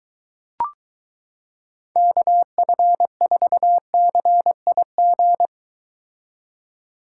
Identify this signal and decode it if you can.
Morse code, heard as a keyed tone, about 23 wpm, 700 Hz. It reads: KF4CIG